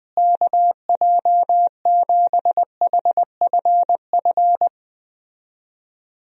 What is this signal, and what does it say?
Morse code, heard as a keyed tone, about 21 wpm, 700 Hz. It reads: KJ7HFF
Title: USA Call Signs - 6 Characters - Repeated 3x - 20wpm